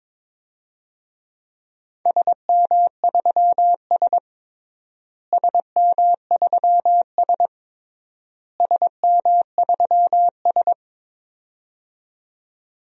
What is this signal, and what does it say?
Morse code, heard as a keyed tone, about 22 wpm, 700 Hz. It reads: SM3S SM3S SM3S
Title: DXCC-100 Call Signs - 4 Characters - Repeated 3x - 22wpm